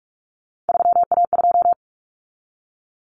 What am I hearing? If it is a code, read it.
Morse code, heard as a keyed tone, about 44 wpm, 700 Hz. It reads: 3A2
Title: Common Cut Numbers - 3-Digit 45wpm